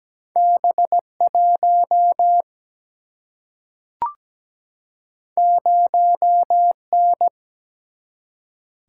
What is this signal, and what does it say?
Morse code, heard as a keyed tone, about 17 wpm, 700 Hz. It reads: B 1 0 N
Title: Single-Letter Single-Number 17wpm